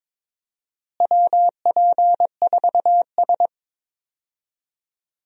Morse code, heard as a keyed tone, about 21 wpm, 700 Hz. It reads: WP4S
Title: USA Call Signs - 4 Characters - Repeated 3x - 22wpm